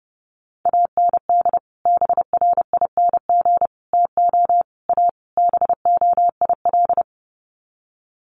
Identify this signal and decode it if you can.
Morse code, heard as a keyed tone, about 30 wpm, 700 Hz. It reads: AND BRING TO A BOIL
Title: Sets of 5 Words Encore 30wpm